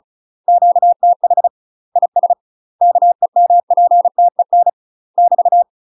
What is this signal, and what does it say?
Morse code, heard as a keyed tone, about 34 wpm, 700 Hz. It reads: QTH IS KEMPTEN <BT>
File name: UK MASTER QSOs TRAINING / 35WPM / UK MASTER QSOs 0011